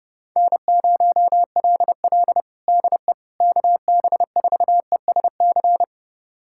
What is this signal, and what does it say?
Morse code, heard as a keyed tone, about 30 wpm, 700 Hz. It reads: N0LL DE KB4ESC
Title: USA Callsign DE Callsign 30wpm